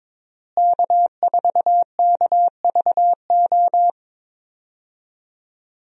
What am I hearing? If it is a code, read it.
Morse code, heard as a keyed tone, about 22 wpm, 700 Hz. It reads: K4KVO
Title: USA Call Signs - 5 Characters - Repeated 3x - 22wpm